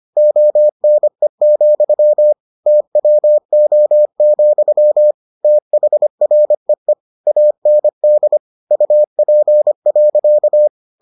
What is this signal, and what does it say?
Morse code, heard as a keyed tone, about 25 wpm, 600 Hz. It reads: ONE, TWO, THREE AND UP.